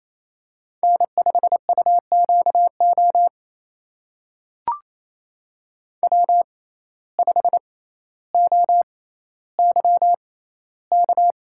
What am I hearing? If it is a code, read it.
Morse code, heard as a keyed tone, about 28 wpm, spaced as for 14 wpm, 700 Hz. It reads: N5UQO W5OYK
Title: USA Call Signs - 5 Characters - 6x Character Spacing - 28wpm